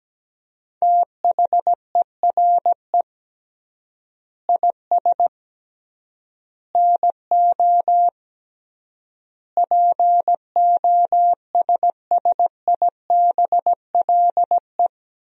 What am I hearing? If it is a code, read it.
Morse code, heard as a keyed tone, about 17 wpm, 700 Hz. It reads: THERE IS NO POSSIBLE
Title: 3x Word-Spacing - Sentences from Top 500 Words - 17wpm